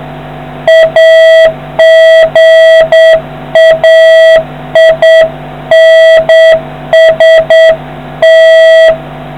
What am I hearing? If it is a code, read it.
Morse code, heard as a keyed tone, about 8 wpm, 650 Hz, with noise. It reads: AGAINST